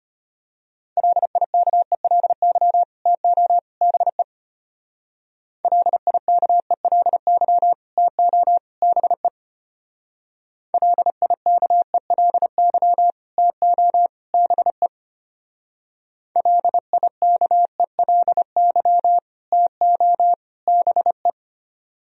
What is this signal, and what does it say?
Morse code, heard as a keyed tone, about 31 wpm, 700 Hz. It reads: LIKELY TO BE LIKELY TO BE LIKELY TO BE LIKELY TO BE
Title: Speed-Racing - Sets of 3 Words 25wpm